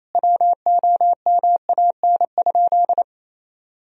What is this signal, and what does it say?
Morse code, heard as a keyed tone, about 28 wpm, 700 Hz. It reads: WOMAN?
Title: Sentences from Top 1000 Words - Review Part 3 - 28wpm